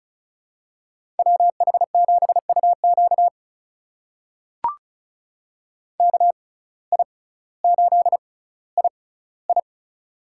Morse code, heard as a keyed tone, about 36 wpm, spaced as for 18 wpm, 700 Hz. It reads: WH7UQ KI8II
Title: USA Call Signs - 5 Characters - 6x Character Spacing - 35wpm